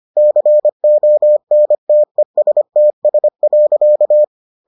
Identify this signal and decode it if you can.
Morse code, heard as a keyed tone, about 25 wpm, 600 Hz. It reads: CONTESTS.